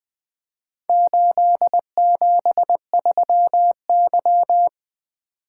Morse code, heard as a keyed tone, about 20 wpm, 700 Hz. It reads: 873Y
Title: Four Letter-Number Combinations - 20wpm